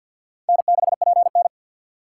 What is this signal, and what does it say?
Morse code, heard as a keyed tone, about 51 wpm, 700 Hz. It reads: N6PN
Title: USA Call Signs - 4 Characters - 3x Character Spacing - 50wpm